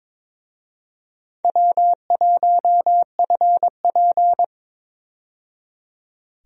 Morse code, heard as a keyed tone, about 21 wpm, 700 Hz. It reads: W1FP
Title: USA Call Signs - 4 Characters - 22wpm